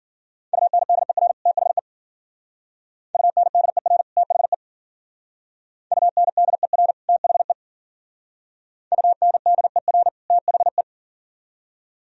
Morse code, heard as a keyed tone, about 51 wpm, 700 Hz. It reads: UNDER THE UNDER THE UNDER THE UNDER THE